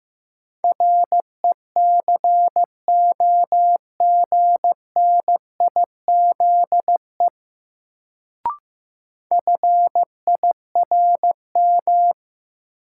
Morse code, heard as a keyed tone, about 15 wpm, 700 Hz. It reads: RECOGNIZE FIRM
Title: Sentences from Top 1400 Words - New Word Drill Alt - 15wpm